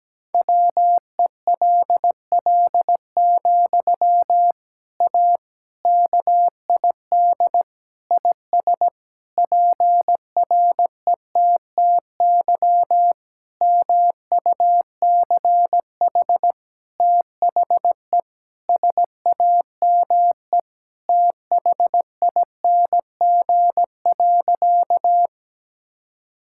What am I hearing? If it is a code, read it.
Morse code, heard as a keyed tone, about 17 wpm, 700 Hz. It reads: WELL, A KID IS PRETTY MUCH THE SAME THING.
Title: Sentences from Top 1600 Words 17wpm